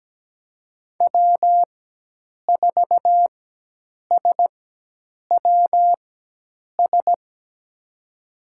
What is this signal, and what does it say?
Morse code, heard as a keyed tone, about 17 wpm, spaced as for 8 wpm, 700 Hz. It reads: W4SWS
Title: USA Call Signs - 5 Characters - 4x Character Spacing - 17wpm